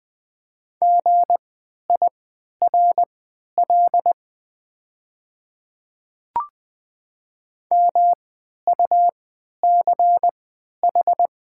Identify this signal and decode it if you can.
Morse code, heard as a keyed tone, about 20 wpm, spaced as for 11 wpm, 700 Hz. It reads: GIRL MUCH
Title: 3x Character Spacing - Top 500 Words - 20wpm